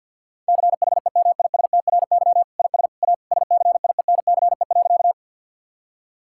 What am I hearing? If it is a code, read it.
Morse code, heard as a keyed tone, about 50 wpm, 700 Hz. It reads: CHEMISTRY IS A SCIENCE.
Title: Sentences from Top 1000 Words - Review Part 6 - 50wpm